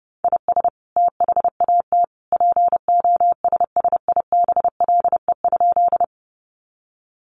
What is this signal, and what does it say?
Morse code, heard as a keyed tone, about 30 wpm, 700 Hz. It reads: IS THAT POSSIBLE?
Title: Sentences from Top 500 Words 30wpm